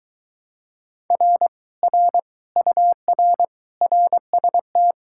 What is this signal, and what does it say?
Morse code, heard as a keyed tone, about 24 wpm, 700 Hz. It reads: R R UR RST